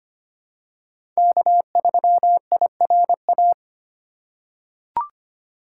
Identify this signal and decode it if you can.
Morse code, heard as a keyed tone, about 25 wpm, 700 Hz. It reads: K3IRA
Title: USA Call Signs - 5 Characters - 2x Character Spacing - 25wpm